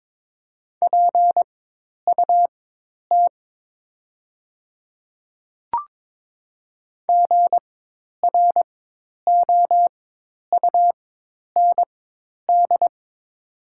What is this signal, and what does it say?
Morse code, heard as a keyed tone, about 22 wpm, spaced as for 10 wpm, 700 Hz. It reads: PUT GROUND
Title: 4x Character Spacing - Top 500 Words - 22wpm